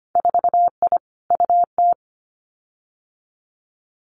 Morse code, heard as a keyed tone, about 25 wpm, 700 Hz. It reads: 4I UT
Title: Winter Field Day Exchanges 25wpm